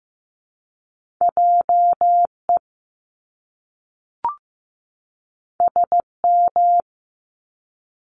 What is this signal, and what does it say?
Morse code, heard as a keyed tone, about 15 wpm, 700 Hz. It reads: JE SM
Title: Top Two Letter Combinations in Words 15wpm